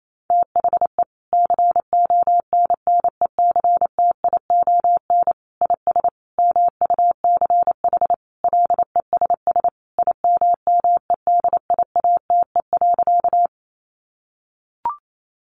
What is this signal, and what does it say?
Morse code, heard as a keyed tone, about 28 wpm, 700 Hz. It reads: THE CONNECTION IS MUCH LESS IMMEDIATE.
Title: Sentences from Top 1300 Words 28wpm